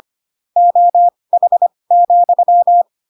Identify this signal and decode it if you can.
Morse code, heard as a keyed tone, about 25 wpm, spaced as for 20 wpm, 700 Hz. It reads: OH,
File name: Pinocchio / Pinocchio ch7 15 25 ik1hge